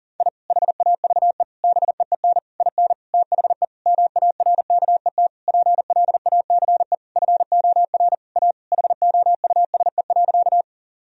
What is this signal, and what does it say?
Morse code, heard as a keyed tone, about 37 wpm, 700 Hz. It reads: I HAVE BEEN IN THE MARKET PLACE FOR A HOUSE.
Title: Sentences from Top 1000 Words - Review Part 3 - 40wpm